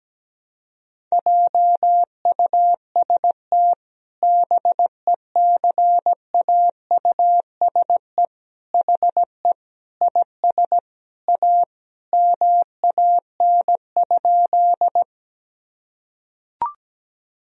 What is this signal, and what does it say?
Morse code, heard as a keyed tone, about 17 wpm, 700 Hz. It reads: JUST BECAUSE HE IS A MAN?
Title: Sentences from Top 1000 Words - Review Part 3 - 17wpm